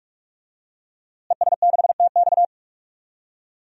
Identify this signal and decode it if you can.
Morse code, heard as a keyed tone, about 46 wpm, 700 Hz. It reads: ES6TX
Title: DXCC-100 Call Signs - 5 Characters - Repeated 2x - 45wpm